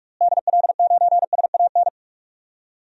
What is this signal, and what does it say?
Morse code, heard as a keyed tone, about 46 wpm, 690 Hz. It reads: DL9SAN
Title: DXCC-100 Call Signs - 6 Characters - Repeated 2x - 45wpm